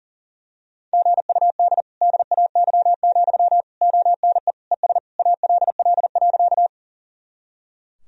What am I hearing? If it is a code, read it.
Morse code, heard as a keyed tone, about 40 wpm, 700 Hz. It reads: GUD DAY, ONE ES ALL.
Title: Sentences from Top 200 Words and Abbreviations 40wpm